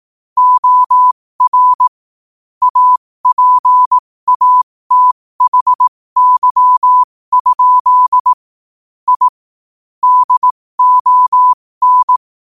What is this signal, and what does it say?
Morse code, heard as a keyed tone, about 18 wpm, spaced as for 15 wpm, 1000 Hz. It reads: OR APATHY? I DON
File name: qotdfiles / 15wpm